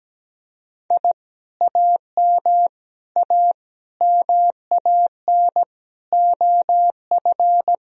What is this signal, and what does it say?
Morse code, heard as a keyed tone, about 17 wpm, 700 Hz. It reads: I AM A MAN OF
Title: Sentences from Top 1000 Words - Review Part 2 - 17wpm